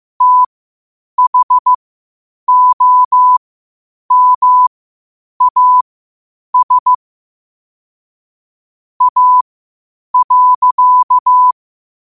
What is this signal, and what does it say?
Morse code, heard as a keyed tone, about 15 wpm, spaced as for 8 wpm, 1000 Hz. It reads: THOMAS A.
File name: qotdfiles / 8wpm